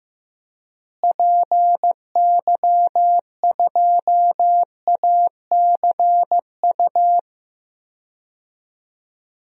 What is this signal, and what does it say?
Morse code, heard as a keyed tone, about 15 wpm, 700 Hz. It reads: PY2ACU
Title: DXCC-100 Call Signs - 6 Characters - Repeated 3x - 15wpm